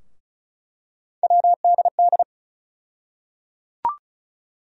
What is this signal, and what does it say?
Morse code, heard as a keyed tone, about 34 wpm, 700 Hz. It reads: WDD
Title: USA Call Sign Postfixes - 3 Letters 35wpm